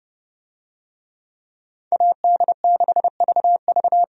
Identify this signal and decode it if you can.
Morse code, heard as a keyed tone, about 30 wpm, 700 Hz. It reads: AD6VV
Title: USA Call Signs - 5 Characters - 30wpm